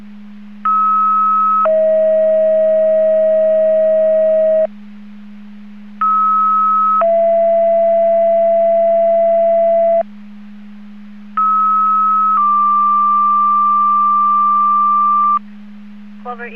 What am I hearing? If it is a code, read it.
Two-tone page, heard as a one-second tone, then a three-second tone, about 1290 then 640 Hz, about 1290 then 670 Hz, about 1290 then 1180 Hz. It is heard from 0.6 to 15.4 s.